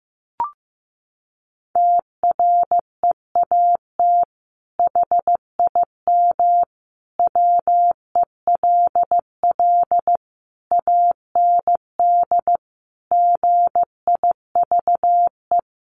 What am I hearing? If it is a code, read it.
Morse code, heard as a keyed tone, about 15 wpm, 700 Hz. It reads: TREAT HIM WELL AND GIVE
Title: Sentences from Top 1300 Words 15wpm